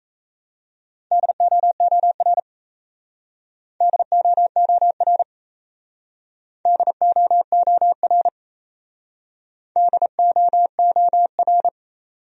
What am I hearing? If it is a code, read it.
Morse code, heard as a keyed tone, about 34 wpm, 700 Hz. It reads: DOOR DOOR DOOR DOOR
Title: Speed-Racing Top 500 Words 28wpm